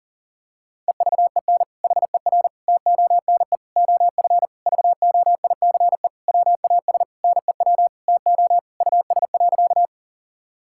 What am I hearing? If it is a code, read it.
Morse code, heard as a keyed tone, about 40 wpm, 700 Hz. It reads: EVEN HER TONE OF VOICE WAS NEW TO US.
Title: Sentences from Top 1600 Words 40wpm